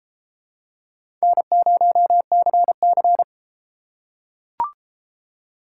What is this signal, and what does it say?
Morse code, heard as a keyed tone, about 32 wpm, 700 Hz. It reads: N0CC